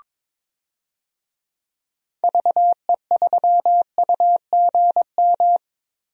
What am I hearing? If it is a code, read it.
Morse code, heard as a keyed tone, about 22 wpm, 700 Hz. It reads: VE3UGM
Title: DXCC-100 Call Signs - 6 Characters - Repeated 2x - 22wpm